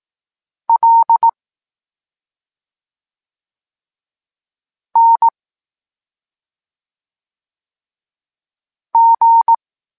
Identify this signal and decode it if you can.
Morse code, heard as a keyed tone, about 18 wpm, 920 Hz. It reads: L N G
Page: Tekst 23.3 – Ciąg dalszy doskonalenia odbioru poznanych liter